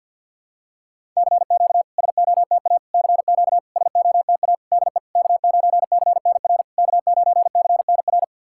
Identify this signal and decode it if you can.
Morse code, heard as a keyed tone, about 50 wpm, 690 Hz. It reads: CQ SOTA CQ SOTA DE K9CNR K9CNR
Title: SOTA QSO Practice 50wpm